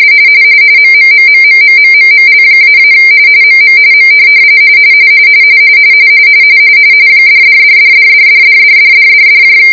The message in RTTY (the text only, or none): MMMMMMMMML-- MM M $$$$$$$$$$$$$